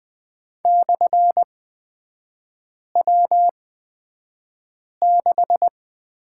Morse code, heard as a keyed tone, about 20 wpm, spaced as for 5 wpm, 700 Hz. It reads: /W6
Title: 224 Lesson 36e - BT - Three Characters 20wpm